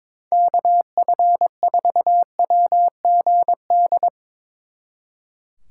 Morse code, heard as a keyed tone, about 24 wpm, 700 Hz. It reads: KF4WGD
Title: Speed Racing - USA 2x3 Call Signs 22wpm